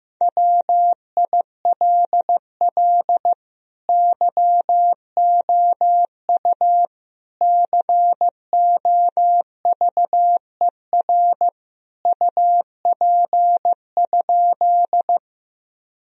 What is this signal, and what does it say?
Morse code, heard as a keyed tone, about 15 wpm, 700 Hz. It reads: WILL YOU COVER UP?
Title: Sentences from Top 1000 Words - Review Part 1 - 15wpm